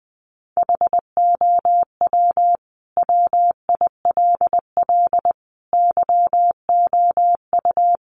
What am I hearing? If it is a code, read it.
Morse code, heard as a keyed tone, about 20 wpm, 700 Hz. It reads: HOW WILL YOU
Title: Sentences from Top 900 Words 20wpm